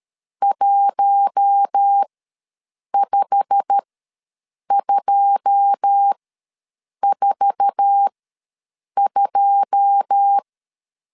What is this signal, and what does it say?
Morse code, heard as a keyed tone, about 13 wpm, spaced as for 7 wpm, 800 Hz. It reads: 15242
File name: LEZIONI / 9 2 ws 12wpm 3406